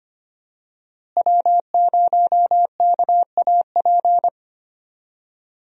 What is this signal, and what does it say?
Morse code, heard as a keyed tone, about 25 wpm, 700 Hz. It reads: W0KAP
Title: USA Call Signs - 5 Characters - Repeated 3x - 25wpm